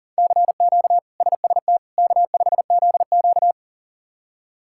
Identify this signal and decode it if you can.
Morse code, heard as a keyed tone, about 40 wpm, 700 Hz. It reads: CQ SST K5ZQ
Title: K1USN SST QSO Practice - 40wpm